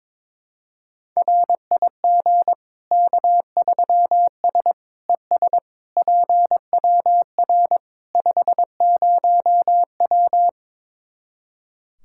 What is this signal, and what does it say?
Morse code, heard as a keyed tone, about 22 wpm, 700 Hz. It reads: RIG K3S ES PWR 50W
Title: Practice QSO - Exchange 2 - Info - 22wpm